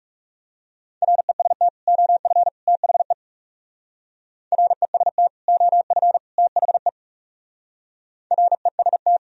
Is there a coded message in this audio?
Yes: Morse code from 1.0 to 9.2 s, heard as a keyed tone, about 38 wpm, 700 Hz. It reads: REST OF THE REST OF THE REST